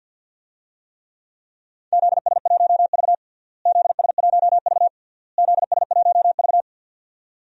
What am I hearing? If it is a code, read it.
Morse code, heard as a keyed tone, about 51 wpm, 700 Hz. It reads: ZS1V ZS1V ZS1V
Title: DXCC-100 Call Signs - 4 Characters - Repeated 3x - 50wpm